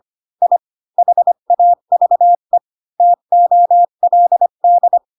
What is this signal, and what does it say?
Morse code, heard as a keyed tone, about 25 wpm, 700 Hz. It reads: I HAVE TOLD
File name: Pinocchio / Pinocchio ch4 20 25 ik1hge